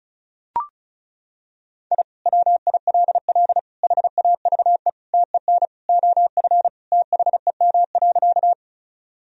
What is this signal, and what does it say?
Morse code, heard as a keyed tone, about 34 wpm, 700 Hz. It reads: I WILL HAVE TEN OF THEM.
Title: Sentences from Top 1000 Words - Review Part 2 - 35wpm